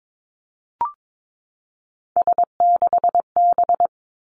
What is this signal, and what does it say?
Morse code, heard as a keyed tone, about 22 wpm, 700 Hz. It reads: S6B
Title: Three Challenging Letter-Number Combinations 22wpm